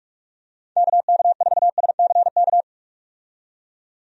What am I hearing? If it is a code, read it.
Morse code, heard as a keyed tone, about 43 wpm, 700 Hz. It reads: KK4SCK
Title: USA Call Signs - 6 Characters - 45wpm